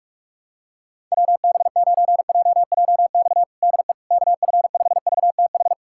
Morse code, heard as a keyed tone, about 41 wpm, 700 Hz. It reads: WB9JJX DE KF5VTH
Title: USA Callsign DE Callsign - 6 Characters - 45wpm